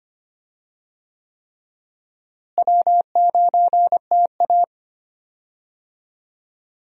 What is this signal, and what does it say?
Morse code, heard as a keyed tone, about 25 wpm, 700 Hz. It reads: W9TA
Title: USA Call Signs - 4 Characters - Repeated 3x - 25wpm